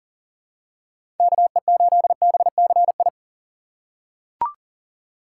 Morse code, heard as a keyed tone, about 37 wpm, 700 Hz. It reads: KE8BCI
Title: USA Call Signs - 6 Characters - 2x Character Spacing - 40wpm